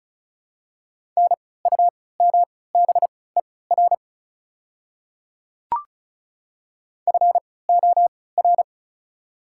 Morse code, heard as a keyed tone, about 36 wpm, spaced as for 20 wpm, 690 Hz. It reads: NUMBER FOR
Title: 3x Character Spacing - Top 300 Words - 35wpm